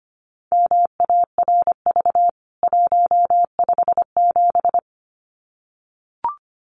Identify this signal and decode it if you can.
Morse code, heard as a keyed tone, about 25 wpm, 700 Hz. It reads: MARV 157
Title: CWT Exchange 25wpm